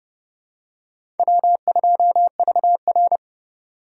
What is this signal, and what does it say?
Morse code, heard as a keyed tone, about 31 wpm, 700 Hz. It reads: W2VR
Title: USA Call Signs - 4 Characters - 3x Character Spacing - 30wpm